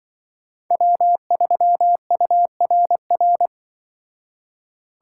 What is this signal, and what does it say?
Morse code, heard as a keyed tone, about 25 wpm, 700 Hz. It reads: W3URR
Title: Speed Racing - USA Call Signs 20wpm